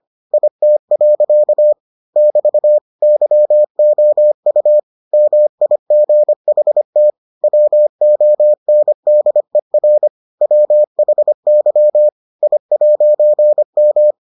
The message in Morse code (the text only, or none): IT. <BT> YOU MIGHT WONDER WHY I'M